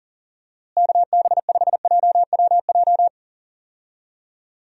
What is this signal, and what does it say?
Morse code, heard as a keyed tone, about 40 wpm, 700 Hz. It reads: KB5JWJ
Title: USA Call Signs - 6 Characters - Repeated 2x - 40wpm